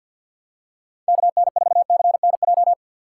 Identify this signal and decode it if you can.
Morse code, heard as a keyed tone, about 50 wpm, 700 Hz. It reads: KN4CNJ
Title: USA Call Signs - 6 Characters - 6x Character Spacing - 50wpm